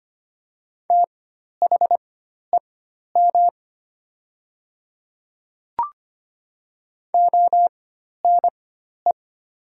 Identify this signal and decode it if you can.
Morse code, heard as a keyed tone, about 25 wpm, spaced as for 12 wpm, 700 Hz. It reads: THEM ONE
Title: 4x Character Spacing - Top 300 Words - 25wpm